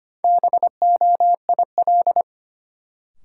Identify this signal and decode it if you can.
Morse code, heard as a keyed tone, about 25 wpm, 700 Hz. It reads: BOIL
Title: Sets of 5 Words Encore - New Word Drill - 25wpm